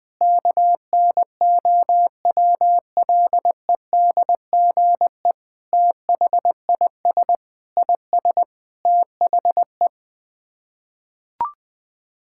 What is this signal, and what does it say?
Morse code, heard as a keyed tone, about 20 wpm, 700 Hz. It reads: KNOWLEDGE THIS IS THE
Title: Sets of 9 Words - 20wpm